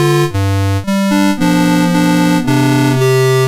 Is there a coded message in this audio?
No.